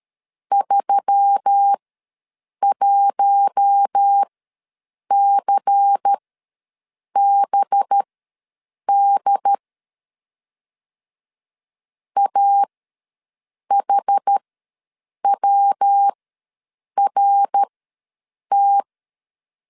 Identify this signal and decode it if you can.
Morse code, heard as a keyed tone, about 13 wpm, spaced as for 7 wpm, 800 Hz. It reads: 31CBD AHWRT
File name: LEZIONI / 5 LBVC / 5 LBVC ws 12wpm 3159